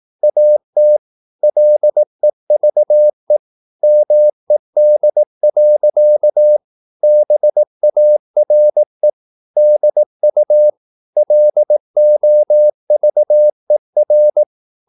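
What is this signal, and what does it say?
Morse code, heard as a keyed tone, about 18 wpm, 600 Hz. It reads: AT LEVE MED. BARE DU LOVER
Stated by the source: Slow (QRS) Morse practice at 18 wpm, CW med Gnister.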